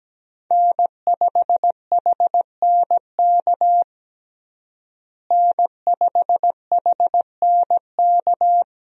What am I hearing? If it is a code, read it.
Morse code, heard as a keyed tone, about 17 wpm, 700 Hz. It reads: N5HNK N5HNK
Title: USA Call Signs - 5 Characters - Repeated 2x - 17wpm